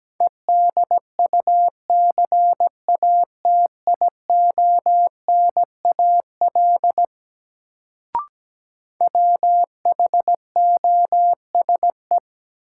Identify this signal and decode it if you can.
Morse code, heard as a keyed tone, about 17 wpm, 700 Hz. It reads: EDUCATIONAL WHOSE
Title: Sentences from Top 1200 Words - New Word Drill - 17wpm